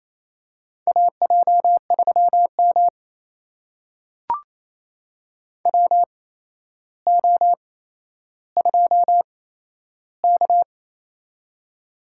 Morse code, heard as a keyed tone, about 28 wpm, spaced as for 12 wpm, 700 Hz. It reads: AJ3M WO2K
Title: USA Call Signs - 4 Characters - 8x Character Spacing - 28wpm